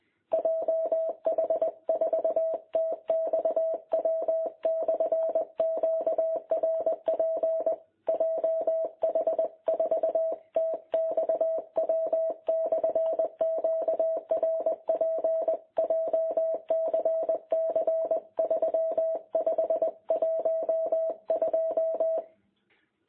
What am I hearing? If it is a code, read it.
Morse code, heard as a keyed tone, about 22 wpm, 660 Hz. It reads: JH4TXW/QRP JH4TXW/QRP JCC 3512
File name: jh8cbh / 2021cw